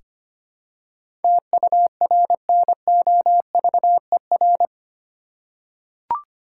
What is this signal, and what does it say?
Morse code, heard as a keyed tone, about 25 wpm, 700 Hz. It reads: TURNOVER
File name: Eight Letter Words - 25wpm